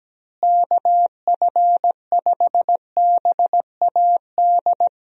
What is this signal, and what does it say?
Morse code, heard as a keyed tone, about 17 wpm, 700 Hz. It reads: KF5BAD
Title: USA Call Signs - 6 Characters - Repeated 2x - 17wpm